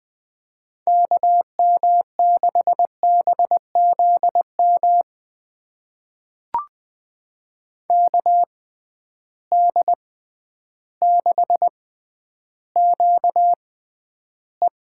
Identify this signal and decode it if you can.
Morse code, heard as a keyed tone, about 20 wpm, spaced as for 11 wpm, 700 Hz. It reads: KM6BZM KD6QE
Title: USA Call Signs - 6 Characters - 6x Character Spacing - 20wpm